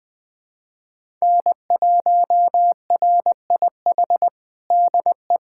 Morse code, heard as a keyed tone, about 20 wpm, 700 Hz. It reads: N1RIH DE
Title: USA Callsign DE Callsign 20wpm